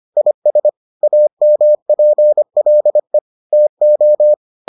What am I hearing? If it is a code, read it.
Morse code, heard as a keyed tone, about 25 wpm, 600 Hz. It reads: IS AMPLE TO